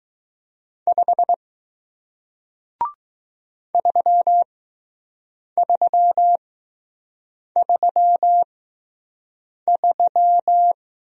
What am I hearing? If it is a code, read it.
Morse code, heard as a keyed tone, about 20 wpm, spaced as for 6 wpm, 700 Hz. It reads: 5 3333